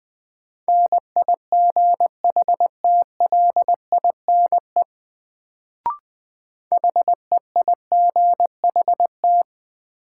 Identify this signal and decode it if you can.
Morse code, heard as a keyed tone, about 20 wpm, 700 Hz. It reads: NIGHTLINE HEIGHT
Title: Sets of 5 Words - New Word Drill - 20wpm